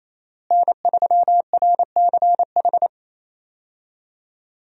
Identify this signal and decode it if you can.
Morse code, heard as a keyed tone, about 28 wpm, 700 Hz. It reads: N3RCH
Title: USA Call Signs - 5 Characters - Repeated 3x - 28wpm